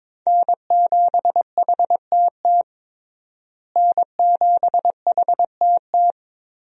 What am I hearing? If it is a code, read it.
Morse code, heard as a keyed tone, about 22 wpm, 700 Hz. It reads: N7HTT N7HTT
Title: USA Call Signs - 5 Characters - Repeated 2x - 22wpm